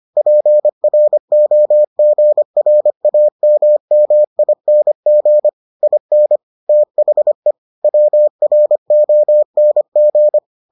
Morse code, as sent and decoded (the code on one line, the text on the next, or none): .--. .-. --- --. .-. .- -- -- .. -. --. / .. -. / - .... / .-- .-. --- -. --.
PROGRAMMING IN THE WRONG